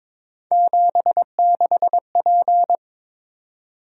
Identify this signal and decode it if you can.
Morse code, heard as a keyed tone, about 22 wpm, 700 Hz. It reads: 76P
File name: practice / 22wpm / Three Letter-Number Combinations - 22wpm